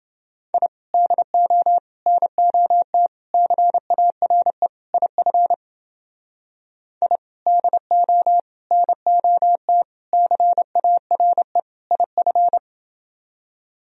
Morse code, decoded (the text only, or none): I DO NOT CARE IF I DO NOT CARE IF